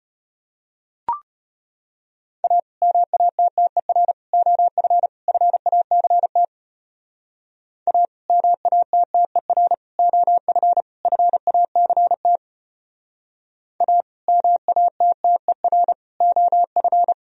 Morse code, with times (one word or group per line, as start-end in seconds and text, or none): 2.4-2.6 A
2.8-4.1 MATTER
4.3-5.1 OF
5.3-6.4 FACT
7.9-8.1 A
8.3-9.7 MATTER
10.0-10.8 OF
11.1-12.4 FACT
13.8-14.0 A
14.3-15.9 MATTER
16.2-17.1 OF